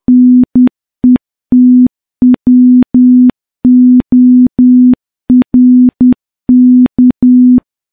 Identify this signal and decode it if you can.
Morse code, heard as a keyed tone, about 10 wpm, 250 Hz. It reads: NETWORK
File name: morse-training-10WPM